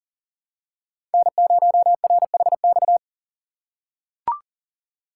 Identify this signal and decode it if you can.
Morse code, heard as a keyed tone, about 38 wpm, 690 Hz. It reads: N0RHX
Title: USA Call Signs - 5 Characters - 3x Character Spacing - 40wpm